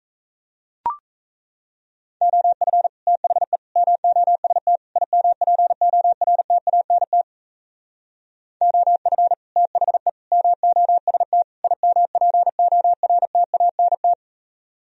Morse code, heard as a keyed tone, about 34 wpm, 700 Hz. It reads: OF THE MOST IMPORTANT OF THE MOST IMPORTANT
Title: Speed-Racing - Sets of 4 Words 28wpm